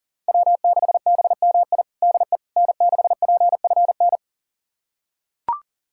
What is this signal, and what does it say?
Morse code, heard as a keyed tone, about 40 wpm, 700 Hz. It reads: W6BMI DE N6PFN
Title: USA Callsign DE Callsign - 5 Characters - 40wpm